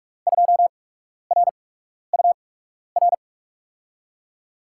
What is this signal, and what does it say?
Morse code, heard as a keyed tone, about 42 wpm, spaced as for 12 wpm, 700 Hz. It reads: 2RUR
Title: USA Call Signs - 6 Characters - 8x Character Spacing - 45wpm